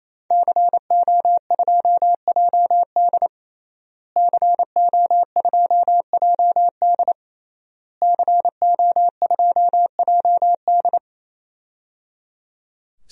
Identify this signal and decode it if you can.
Morse code, heard as a keyed tone, about 28 wpm, 700 Hz. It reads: CO2JD CO2JD CO2JD
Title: DXCC-100 Call Signs - 5 Characters - Repeated 3x - 28wpm